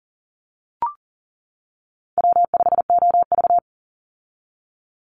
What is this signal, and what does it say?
Morse code, heard as a keyed tone, about 40 wpm, 700 Hz. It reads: W5OV
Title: Call Signs - 40wpm